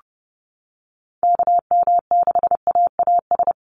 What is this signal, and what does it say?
Morse code, heard as a keyed tone, about 30 wpm, 700 Hz. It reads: KM6AAS
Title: USA Call Signs 30wpm